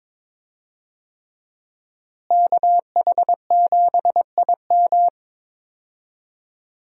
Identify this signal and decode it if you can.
Morse code, heard as a keyed tone, about 22 wpm, 700 Hz. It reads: KH7IM
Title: USA Call Signs - 5 Characters - Repeated 3x - 22wpm